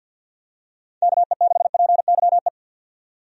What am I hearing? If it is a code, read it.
Morse code, heard as a keyed tone, about 52 wpm, 700 Hz. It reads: KE6PYE
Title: USA Call Signs - 6 Characters - 50wpm